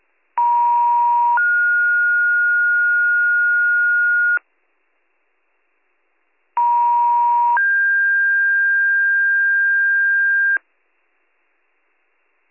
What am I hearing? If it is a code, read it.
Two-tone page, heard as a one-second tone, then a three-second tone, about 950 then 1430 Hz, about 950 then 1600 Hz. It is heard from 0.4 to 10.6 s.